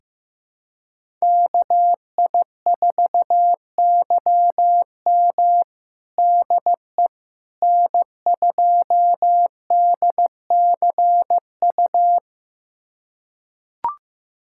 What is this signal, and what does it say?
Morse code, heard as a keyed tone, about 15 wpm, 700 Hz. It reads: KI4YM DE N2DCU
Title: USA Callsign DE Callsign - 5 Characters - 15wpm